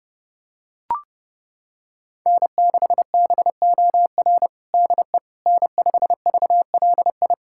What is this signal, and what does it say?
Morse code, heard as a keyed tone, about 29 wpm, 700 Hz. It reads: N6BOR DE N5VLI
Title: USA Callsign DE Callsign - 5 Characters - 30wpm